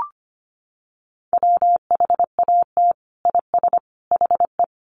Morse code, heard as a keyed tone, about 25 wpm, 700 Hz. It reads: WHAT IS HE
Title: Sentences from Top 100 Words 25wpm